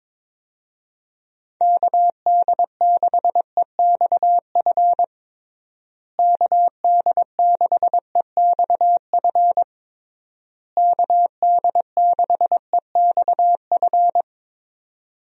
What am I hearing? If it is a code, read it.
Morse code, heard as a keyed tone, about 22 wpm, 700 Hz. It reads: KD6EXF KD6EXF KD6EXF
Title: USA Call Signs - 6 Characters - Repeated 3x - 22wpm